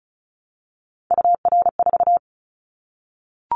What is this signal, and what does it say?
Morse code, heard as a keyed tone, about 34 wpm, 700 Hz. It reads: UR4
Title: Callsign Prefixes for Top 25 DXCC - Repeated 3x - 35wpm